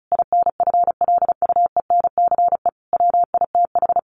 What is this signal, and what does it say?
Morse code, heard as a keyed tone, about 34 wpm, 700 Hz. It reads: INFLUENCE WITH
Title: Sentences from Top 1100 Words 35wpm